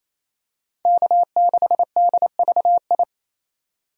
Morse code, heard as a keyed tone, about 28 wpm, 700 Hz. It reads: K6DVI
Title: USA Call Signs - 5 Characters - 28wpm